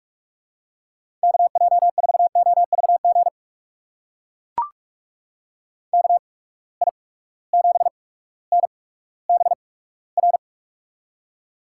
Morse code, heard as a keyed tone, about 43 wpm, spaced as for 19 wpm, 700 Hz. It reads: KJ4OVG KI7NBR
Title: USA Call Signs - 6 Characters - 8x Character Spacing - 45wpm